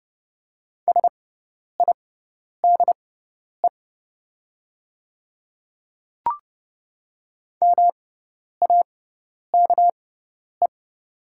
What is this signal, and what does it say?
Morse code, heard as a keyed tone, about 30 wpm, spaced as for 10 wpm, 700 Hz. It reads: SIDE MAKE
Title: 6x Character Spacing - Top 100 Words - 30wpm